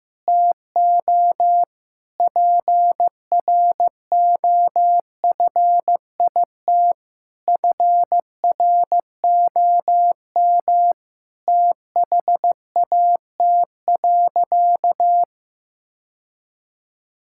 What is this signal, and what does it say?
Morse code, heard as a keyed tone, about 15 wpm, 700 Hz. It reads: TO PROFIT FROM THAT.
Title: Sentences from Top 1000 Words - Review Part 5 - 15wpm